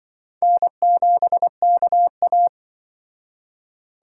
Morse code, heard as a keyed tone, about 23 wpm, 700 Hz. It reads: N7KA
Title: Speed Racing - USA 1x2 Call Signs 20wpm